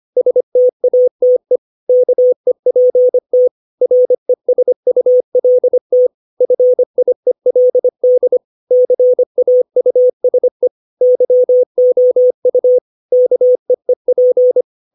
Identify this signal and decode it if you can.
Morse code, heard as a keyed tone, about 25 wpm, 500 Hz. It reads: STATE KEPT RESULT FIELD CAUSE YOU KEEP
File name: 632 words / 632 Words 25WPM